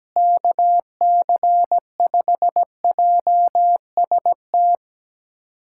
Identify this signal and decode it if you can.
Morse code, heard as a keyed tone, about 17 wpm, 700 Hz. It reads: KC5JST
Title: USA Callsign DE Callsign - 6 Characters - 17wpm